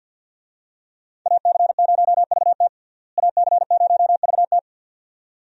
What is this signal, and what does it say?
Morse code, heard as a keyed tone, about 52 wpm, 700 Hz. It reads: AC0VT AC0VT
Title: USA Call Signs - 5 Characters - Repeated 2x - 50wpm